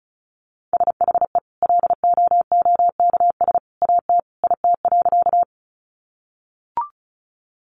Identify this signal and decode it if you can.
Morse code, heard as a keyed tone, about 34 wpm, 700 Hz. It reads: SHE LOOKS AT IT.
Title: Additional Easy Phrases 35wpm